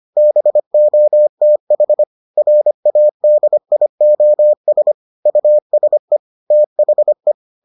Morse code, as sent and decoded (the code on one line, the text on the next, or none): -... --- - .... / .-. .- -.. .. --- ... / ..- ... / - .... .
BOTH RADIOS USE THE